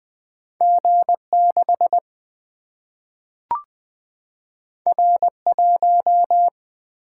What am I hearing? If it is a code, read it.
Morse code, heard as a keyed tone, about 20 wpm, 700 Hz. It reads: G 6 R 1
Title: Single-Letter Single-Number 20wpm